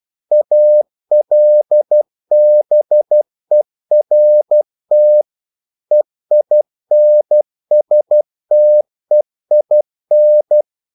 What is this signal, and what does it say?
Morse code, heard as a keyed tone, about 12 wpm, 600 Hz. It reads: ALBERT EINSTEIN
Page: English quotations in CW (telegraphy - morse)